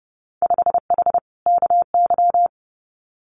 Morse code, heard as a keyed tone, about 30 wpm, 700 Hz. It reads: 5H KY
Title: Winter Field Day Exchanges Encore 30wpm